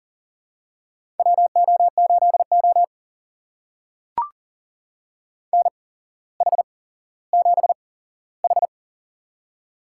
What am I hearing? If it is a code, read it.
Morse code, heard as a keyed tone, about 40 wpm, spaced as for 17 wpm, 700 Hz. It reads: WO8O NH7H
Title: USA Call Signs - 4 Characters - 8x Character Spacing - 40wpm